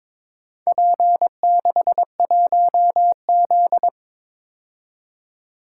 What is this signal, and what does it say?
Morse code, heard as a keyed tone, about 22 wpm, 700 Hz. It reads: P61Z
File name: practice / 22wpm / Four Letter-Number Combinations - 22wpm